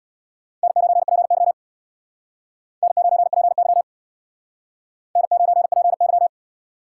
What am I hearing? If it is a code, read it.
Morse code, heard as a keyed tone, about 65 wpm, 700 Hz. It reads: N9PY N9PY N9PY